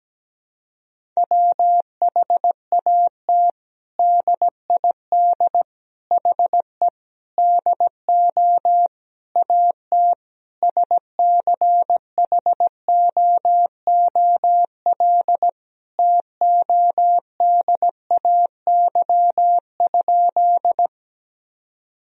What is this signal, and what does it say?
Morse code, heard as a keyed tone, about 17 wpm, 700 Hz. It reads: WHAT DID HE DO AT SCHOOL TODAY?
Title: Sentences from Top 1000 Words - Review Part 1 - 17wpm